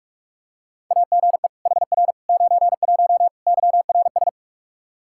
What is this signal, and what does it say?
Morse code, heard as a keyed tone, about 46 wpm, 700 Hz. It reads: AGE HR 91 YRS
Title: Practice QSO - Exchange 3 - More Info - 45wpm